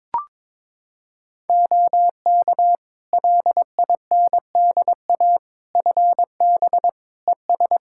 Morse code, heard as a keyed tone, about 22 wpm, 700 Hz. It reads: OK LINDA FB ES
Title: Practice QSO - Exchange 2 - Info - 22wpm